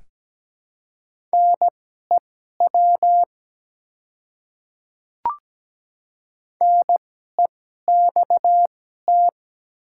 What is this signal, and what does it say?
Morse code, heard as a keyed tone, about 17 wpm, spaced as for 12 wpm, 690 Hz. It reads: NEW NEXT
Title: 2x Character Spacing - Top 300 Words - 17wpm